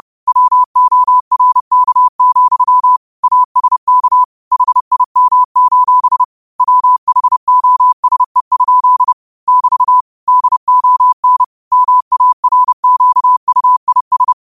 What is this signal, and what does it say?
Morse code, heard as a keyed tone, about 30 wpm, 1000 Hz. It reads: WORK, ASK HIM: WHOSE? <BT> DON MARQUIS
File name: qotdfiles / 30wpm